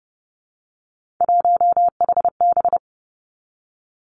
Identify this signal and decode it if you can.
Morse code, heard as a keyed tone, about 30 wpm, 700 Hz. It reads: 1HB
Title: Three Challenging Letter-Number Combinations 30wpm